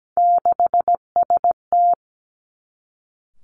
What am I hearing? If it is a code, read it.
Morse code, heard as a keyed tone, about 17 wpm, 700 Hz. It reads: 6ST
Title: Three Challenging Letter-Number Combinations 17wpm